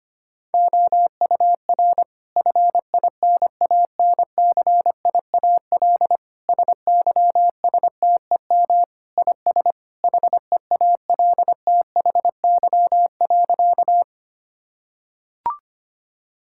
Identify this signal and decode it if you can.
Morse code, heard as a keyed tone, about 25 wpm, 700 Hz. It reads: OUR FINANCIAL SYSTEM IS HEALTHY.
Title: Sentences from Top 1000 Words - Review Part 5 - 25wpm